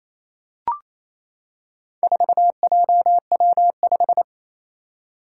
Morse code, heard as a keyed tone, about 28 wpm, 700 Hz. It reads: 4JW5
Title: Four Challenging Letter-Number Combinations 28wpm